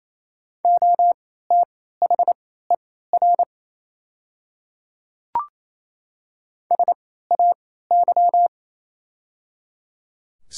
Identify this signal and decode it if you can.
Morse code, heard as a keyed tone, about 28 wpm, spaced as for 16 wpm, 700 Hz. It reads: OTHER SAY
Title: 3x Character Spacing - Top 200 Words - 28wpm